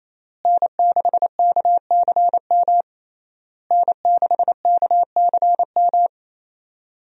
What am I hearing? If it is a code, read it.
Morse code, heard as a keyed tone, about 28 wpm, 700 Hz. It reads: N6KCM N6KCM
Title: USA Call Signs - 5 Characters - Repeated 3x - 28wpm